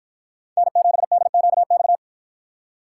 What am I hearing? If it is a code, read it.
Morse code, heard as a keyed tone, about 53 wpm, 700 Hz. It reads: N7DQX